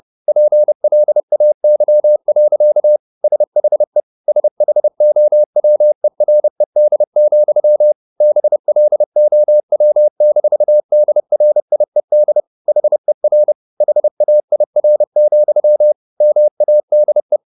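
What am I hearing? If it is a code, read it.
Morse code, heard as a keyed tone, about 30 wpm, 600 Hz. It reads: PLAY. SHE SHOWERED, BLOW-DRIED HER HAIR, MADE
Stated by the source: Morse code at 30 wpm, the novel Never by Ken Follett in English, from CW med Gnister.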